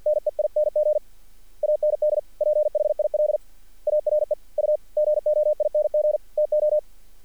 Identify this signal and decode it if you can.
Morse code, heard as a keyed tone, about 50 wpm, 600 Hz. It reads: NEING AND PHIL ARE U GOING TO
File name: archived / CRSnet